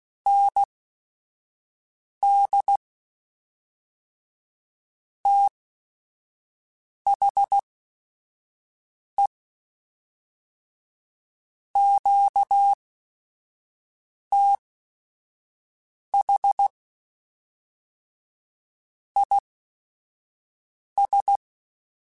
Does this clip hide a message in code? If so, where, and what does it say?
Yes: Morse code from 0.3 to 21.4 s, heard as a keyed tone, about 16 wpm, spaced as for 5 wpm, 800 Hz. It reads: ND THE QTH IS